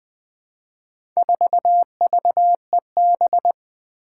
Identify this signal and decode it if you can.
Morse code, heard as a keyed tone, about 20 wpm, 700 Hz. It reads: 4VEB
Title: Four Challenging Letter-Number Combinations 20wpm